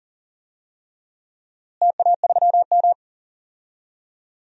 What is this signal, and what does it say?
Morse code, heard as a keyed tone, about 38 wpm, 700 Hz. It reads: TA3M
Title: DXCC-100 Call Signs - 4 Characters - Repeated 3x - 40wpm